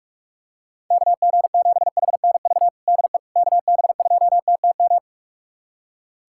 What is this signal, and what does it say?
Morse code, heard as a keyed tone, about 46 wpm, 690 Hz. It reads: KG7HNV DE KB2TTM
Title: USA Callsign DE Callsign - 6 Characters - 45wpm